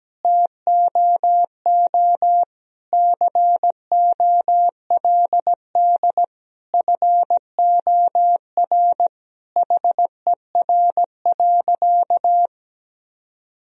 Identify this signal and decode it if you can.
Morse code, heard as a keyed tone, about 17 wpm, 700 Hz. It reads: TOO COLD FOR HER.
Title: Sentences from Top 1000 Words - Review Part 4 - 17wpm